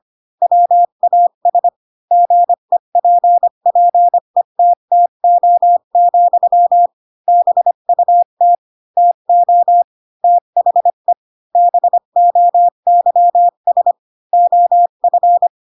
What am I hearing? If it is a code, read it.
Morse code, heard as a keyed tone, about 25 wpm, 700 Hz. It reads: WAS GEPPETTO, BUT TO THE BOYS OF
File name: Pinocchio / Pinocchio ch2 20 25 ik1hge